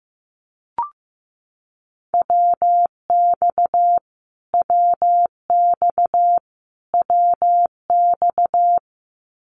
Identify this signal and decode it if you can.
Morse code, heard as a keyed tone, about 15 wpm, 700 Hz. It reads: WX WX WX
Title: Abbreviations Repeated 3x - 15wpm